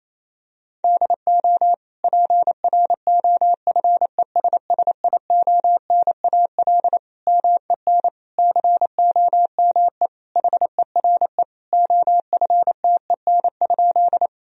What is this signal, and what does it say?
Morse code, heard as a keyed tone, about 28 wpm, 700 Hz. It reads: DO PROFESSIONAL MEN COME HERE OFTEN?
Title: Sentences from Top 1000 Words - Review Part 4 - 28wpm